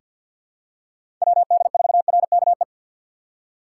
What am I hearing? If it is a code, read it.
Morse code, heard as a keyed tone, about 50 wpm, 700 Hz. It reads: WD4RKE
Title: USA Call Signs - 6 Characters - 3x Character Spacing - 50wpm